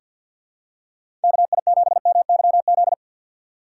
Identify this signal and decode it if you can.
Morse code, heard as a keyed tone, about 47 wpm, 700 Hz. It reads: KI7MYZ
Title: USA Call Signs - 6 Characters - 4x Character Spacing - 50wpm